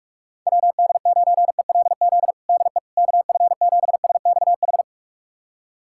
Morse code, heard as a keyed tone, about 42 wpm, 700 Hz. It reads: WD9ELZ DE KF7SXH
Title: USA Callsign DE Callsign - 6 Characters - 45wpm